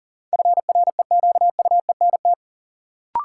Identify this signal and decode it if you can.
Morse code, heard as a keyed tone, about 37 wpm, 700 Hz. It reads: FREQUENT
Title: Sentences from Top 1400 Words - New Word Drill - 40wpm